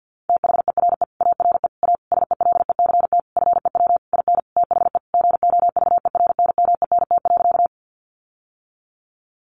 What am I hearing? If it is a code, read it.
Morse code, heard as a keyed tone, about 48 wpm, 700 Hz. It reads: THERE ARE A SELECT FEW IN THE GOVERNMENT.
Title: Sentences from Top 1300 Words 50wpm